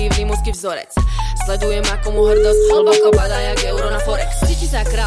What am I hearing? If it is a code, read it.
Ringback tone (Europe, 425 Hz), with noise.